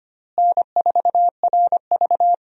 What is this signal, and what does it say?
Morse code, heard as a keyed tone, about 25 wpm, 700 Hz. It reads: N4RV
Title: USA Call Signs - 4 Characters - Repeated 3x - 25wpm